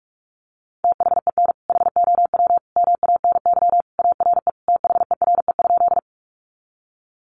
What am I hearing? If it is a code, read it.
Morse code, heard as a keyed tone, about 44 wpm, 700 Hz. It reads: THEN HOW MANY ARE THERE?
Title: Sentences from Top 100 Words 45wpm